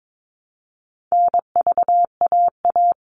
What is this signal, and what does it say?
Morse code, heard as a keyed tone, about 22 wpm, 700 Hz. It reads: NVAA
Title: All Cut Numbers - 4-Digit 22wpm